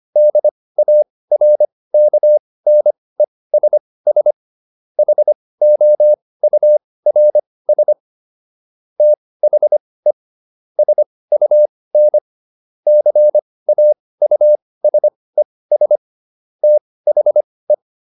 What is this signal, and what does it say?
Morse code, heard as a keyed tone, about 25 wpm, spaced as for 18 wpm, 600 Hz. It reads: DARKNESS HOURS THE SUN CAUSES THE
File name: SS211 18